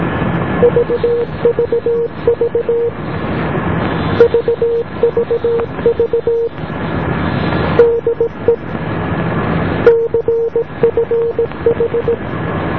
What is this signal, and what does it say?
Morse code, heard as a keyed tone, about 17 wpm, 470 Hz, with noise. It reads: VVV VVV DE CFH